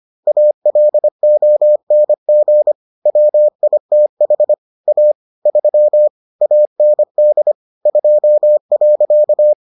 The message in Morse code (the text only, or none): ALONG WITH A 3 AND 2.